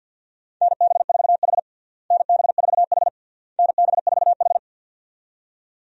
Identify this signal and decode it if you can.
Morse code, heard as a keyed tone, about 48 wpm, 700 Hz. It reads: NB4H NB4H NB4H
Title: USA Call Signs - 4 Characters - Repeated 3x - 50wpm